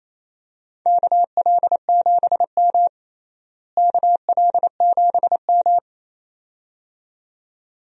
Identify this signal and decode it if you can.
Morse code, heard as a keyed tone, about 28 wpm, 700 Hz. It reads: KL7M KL7M
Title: DXCC-100 Call Signs - 4 Characters - Repeated 2x - 28wpm